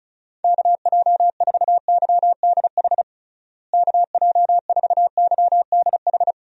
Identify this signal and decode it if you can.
Morse code, heard as a keyed tone, about 36 wpm, 700 Hz. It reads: KJ4YDH KJ4YDH
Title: USA Call Signs - 6 Characters - Repeated 2x - 35wpm